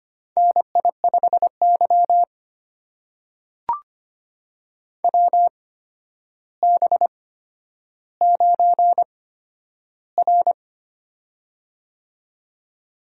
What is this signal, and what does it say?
Morse code, heard as a keyed tone, about 25 wpm, spaced as for 11 wpm, 700 Hz. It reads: NI5Y WB9R
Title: USA Call Signs - 8x Character Spacing - 25wpm